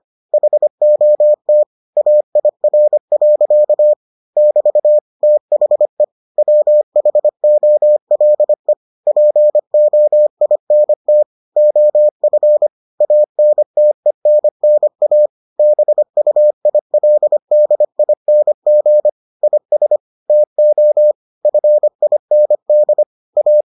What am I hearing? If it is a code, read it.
Morse code, heard as a keyed tone, about 25 wpm, 600 Hz. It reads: HOT AIR. <BT> THE WHOLE POINT OF ANTENNA BUILDING IS TO FIND A